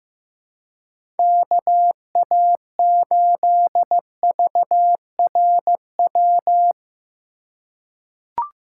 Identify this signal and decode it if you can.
Morse code, heard as a keyed tone, about 15 wpm, 700 Hz. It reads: KA8VRW
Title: USA Call Signs - 6 Characters - 3x Character Spacing - 15wpm